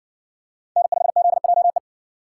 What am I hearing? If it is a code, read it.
Morse code, heard as a keyed tone, about 59 wpm, 700 Hz. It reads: N5ZJE